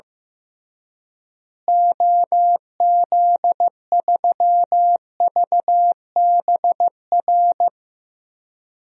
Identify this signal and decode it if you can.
Morse code, heard as a keyed tone, about 16 wpm, 700 Hz. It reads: OZ3VBR
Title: DXCC-100 Call Signs - 6 Characters - Repeated 2x - 15wpm